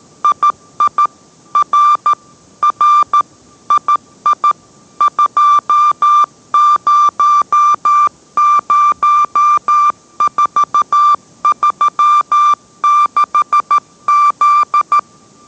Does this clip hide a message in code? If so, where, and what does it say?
Yes: Morse code from 0.2 to 15.0 s, heard as a keyed tone, about 13 wpm, 1180 Hz. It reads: II R R II 200436Z